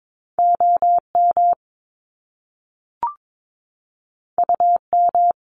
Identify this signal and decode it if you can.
Morse code, heard as a keyed tone, about 22 wpm, 700 Hz. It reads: OM UM
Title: Top Two Letter Combinations in Words 22wpm